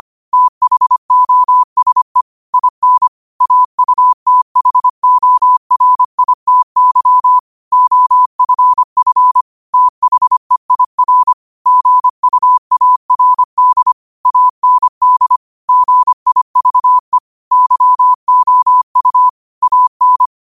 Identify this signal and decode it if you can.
Morse code, heard as a keyed tone, about 25 wpm, 1000 Hz. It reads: THOSE IN AUTHORITY OFF THEIR GUARD AND GIVE YOU AN
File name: qotdfiles / 25wpm